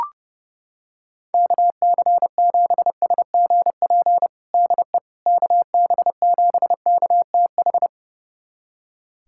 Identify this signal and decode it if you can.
Morse code, heard as a keyed tone, about 32 wpm, 700 Hz. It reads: KC7SGP DE KB7KTH
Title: USA Callsign DE Callsign 30wpm